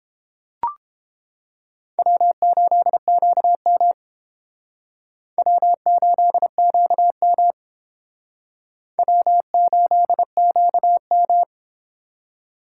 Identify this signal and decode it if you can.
Morse code, heard as a keyed tone, about 29 wpm, 700 Hz. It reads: W8QM W8QM W8QM